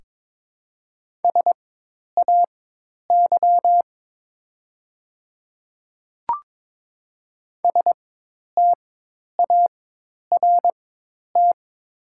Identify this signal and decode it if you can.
Morse code, heard as a keyed tone, about 22 wpm, spaced as for 10 wpm, 700 Hz. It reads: SAY START
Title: 4x Character Spacing - Top 300 Words - 22wpm